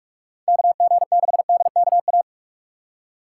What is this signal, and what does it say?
Morse code, heard as a keyed tone, about 46 wpm, 700 Hz. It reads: KG6DKA
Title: USA Call Signs - 6 Characters - 4x Character Spacing - 45wpm